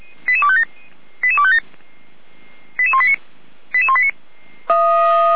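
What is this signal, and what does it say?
Five-tone selective call (ZVEI), heard as quick runs of five tones, tones of about 70 ms. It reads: 88257 88257 88189 88189
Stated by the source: ZVEI five-tone call received by SDR.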